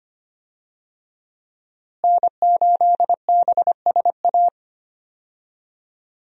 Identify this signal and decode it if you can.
Morse code, heard as a keyed tone, about 25 wpm, 700 Hz. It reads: N8BSA
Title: USA Call Signs - 5 Characters - Repeated 3x - 25wpm